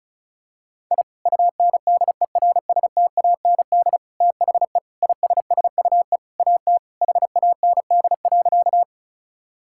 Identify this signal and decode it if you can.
Morse code, heard as a keyed tone, about 34 wpm, 700 Hz. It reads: I UNDERSTAND THE ISSUE AT HAND.
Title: Sentences from Top 1000 Words - Review Part 6 - 35wpm